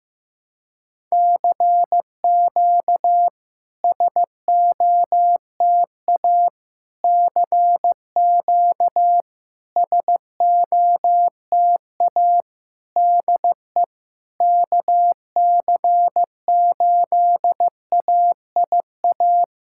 Morse code, heard as a keyed tone, about 16 wpm, 690 Hz. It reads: CQ SOTA CQ SOTA DE KC8AIA
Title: SOTA QSO Practice 15wpm